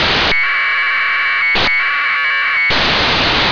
Packet radio (AX.25, as in APRS).
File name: antelsat 0843utc